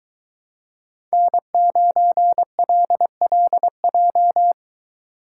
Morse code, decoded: N9LLJ

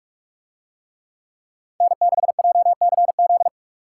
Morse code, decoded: N6JCZ

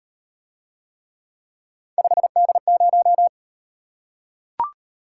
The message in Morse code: 5D0

